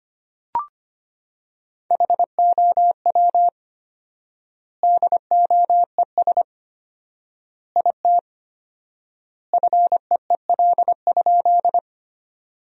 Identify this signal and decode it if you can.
Morse code, heard as a keyed tone, about 25 wpm, 700 Hz. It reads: HOW DOES IT FEEL?
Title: 4x Word-Spacing - Sentences from Top 400 Words - 25wpm